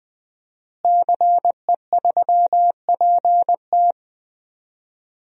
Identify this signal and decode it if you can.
Morse code, heard as a keyed tone, about 21 wpm, 700 Hz. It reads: CE3PT
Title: DXCC-100 Call Signs - 5 Characters - Repeated 2x - 20wpm